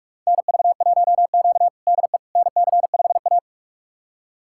Morse code, heard as a keyed tone, about 42 wpm, 700 Hz. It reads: NV1Q DE NC5A
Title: USA Callsign DE Callsign - 4 Characters - 45wpm